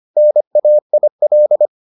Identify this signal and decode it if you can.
Morse code, heard as a keyed tone, about 25 wpm, 600 Hz. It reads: NAIL